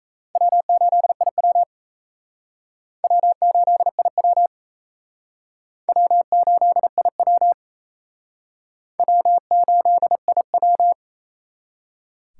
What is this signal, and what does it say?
Morse code, heard as a keyed tone, about 36 wpm, 700 Hz. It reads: W8IW W8IW W8IW W8IW